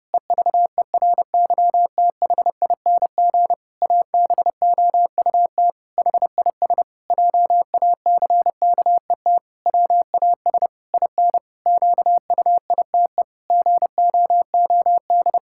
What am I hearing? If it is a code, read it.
Morse code, heard as a keyed tone, about 28 wpm, 700 Hz. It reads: EVERYTHING ABOUT HIS JACKET WAS IN QUITE GOOD
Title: Pangram Challenge v2 - 30wpm